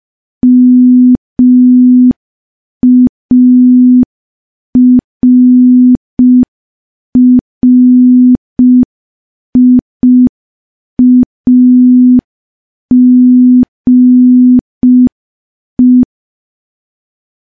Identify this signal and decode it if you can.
Morse code, heard as a keyed tone, about 5 wpm, 250 Hz. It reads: MARRIAGE
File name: morse / morse-training-05WPM